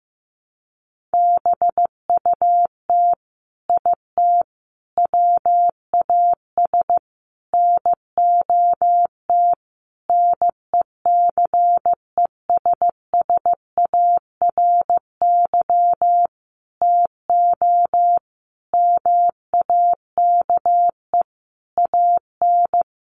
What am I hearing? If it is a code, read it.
Morse code, heard as a keyed tone, about 15 wpm, 700 Hz. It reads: BUT IT WAS NOT NECESSARY TO MAKE AN
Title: Sentences from Top 1200 Words 15wpm